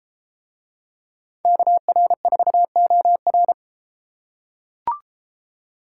Morse code, heard as a keyed tone, about 30 wpm, 700 Hz. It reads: KR4OR